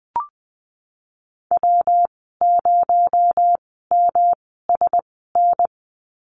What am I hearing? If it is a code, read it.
Morse code, heard as a keyed tone, about 20 wpm, spaced as for 14 wpm, 700 Hz. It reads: W0MSN
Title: 245 Lesson 39g - Letter z - Call Signs 20wpm